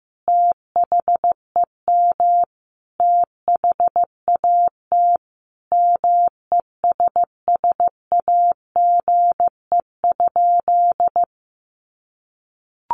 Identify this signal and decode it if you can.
Morse code, heard as a keyed tone, about 15 wpm, 700 Hz. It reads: THEM THAT MESSAGE?
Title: Sentences from Top 1100 Words 15wpm